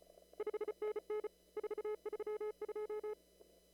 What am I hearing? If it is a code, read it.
Morse code, heard as a keyed tone, about 34 wpm, 430 Hz. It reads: 5NN 432
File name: CW / kt5j wpx cw-op-k5pi